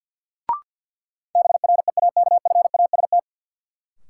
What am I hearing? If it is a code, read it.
Morse code, heard as a keyed tone, about 52 wpm, 700 Hz. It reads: BREAKFAST
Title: Sentences from Top 1500 Words - New Word Drill - 50wpm